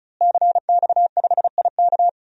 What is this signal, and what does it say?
Morse code, heard as a keyed tone, about 34 wpm, 700 Hz. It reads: CX5IK
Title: DXCC-100 Call Signs - 5 Characters - Repeated 2x - 35wpm